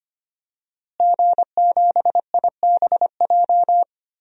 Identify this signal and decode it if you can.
Morse code, heard as a keyed tone, about 25 wpm, 700 Hz. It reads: G7IBJ
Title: DXCC-100 Call Signs - 5 Characters - Repeated 3x - 25wpm